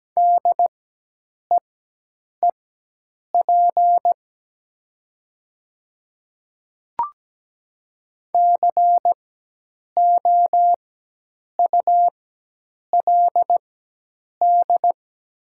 Morse code, heard as a keyed tone, about 17 wpm, spaced as for 8 wpm, 700 Hz. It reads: DEEP COULD
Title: 4x Character Spacing - Top 500 Words - 17wpm